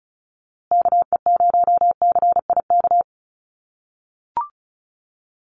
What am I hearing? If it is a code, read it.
Morse code, heard as a keyed tone, about 34 wpm, 700 Hz. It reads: KE0CIK
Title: USA Call Signs Repeated 3x 35wpm